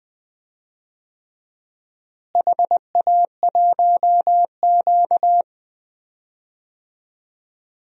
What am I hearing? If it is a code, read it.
Morse code, heard as a keyed tone, about 20 wpm, 700 Hz. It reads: HA1Q
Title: DXCC-100 Call Signs - 4 Characters - Repeated 3x - 20wpm